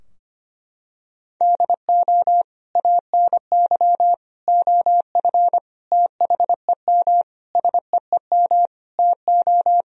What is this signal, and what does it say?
Morse code, heard as a keyed tone, about 25 wpm, 700 Hz. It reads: DO ANY OF THEM SEEM TO
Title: Sentences from Top 1000 Words - Review Part 3 - 25wpm